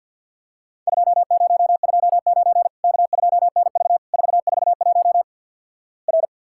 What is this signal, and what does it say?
Morse code, heard as a keyed tone, about 50 wpm, 700 Hz. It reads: 2029 K2NV 441 R
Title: Worked All Europe DX Contest - QTC Exchange - 50wpm